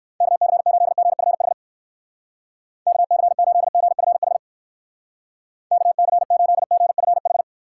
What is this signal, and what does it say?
Morse code, heard as a keyed tone, about 57 wpm, 700 Hz. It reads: KC8GFH KC8GFH KC8GFH